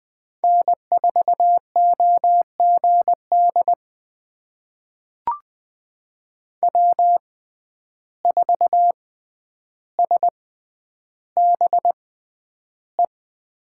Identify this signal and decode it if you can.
Morse code, heard as a keyed tone, about 20 wpm, spaced as for 10 wpm, 700 Hz. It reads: N4OGD W4SBE
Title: USA Call Signs - 6x Character Spacing - 20wpm